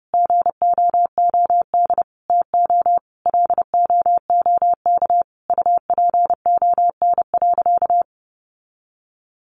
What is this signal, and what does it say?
Morse code, heard as a keyed tone, about 30 wpm, 700 Hz. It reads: GOOD TO LOOK UPON.
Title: Sentences from Top 1400 Words 30wpm